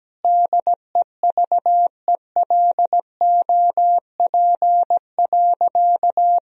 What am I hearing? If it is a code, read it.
Morse code, heard as a keyed tone, about 17 wpm, 700 Hz. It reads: DEVELOP.
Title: Sentences from Top 1000 Words - Review Part 2 - 17wpm